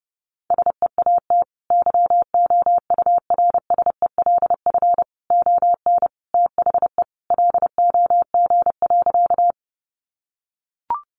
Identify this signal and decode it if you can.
Morse code, heard as a keyed tone, about 30 wpm, 700 Hz. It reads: SEAT YOURSELF ON THE LOG.
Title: Sentences from Top 1500 Words 30wpm